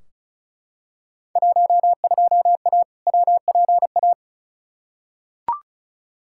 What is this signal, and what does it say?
Morse code, heard as a keyed tone, about 34 wpm, 700 Hz. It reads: 12A WPA
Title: Field Day Exchanges Encore - 35wpm